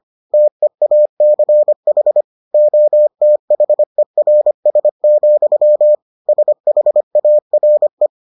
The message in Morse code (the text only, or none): TEACH OTHERS, SHARE